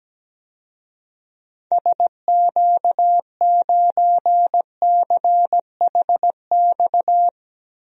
Morse code, heard as a keyed tone, about 17 wpm, 700 Hz. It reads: SQ9CHX
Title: DXCC-100 Call Signs - 6 Characters - 17wpm